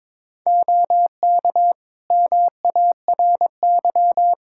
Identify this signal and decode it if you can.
Morse code, heard as a keyed tone, about 22 wpm, 700 Hz. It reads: OK MARY
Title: Practice QSO - Exchange 3 - More Info - 22wpm